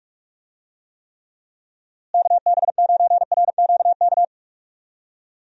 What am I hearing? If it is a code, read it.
Morse code, heard as a keyed tone, about 46 wpm, 700 Hz. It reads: KB9RQK
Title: USA Call Signs - 6 Characters - Repeated 2x - 45wpm